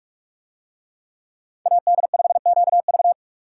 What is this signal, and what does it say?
Morse code, heard as a keyed tone, about 44 wpm, 700 Hz. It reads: AD5QV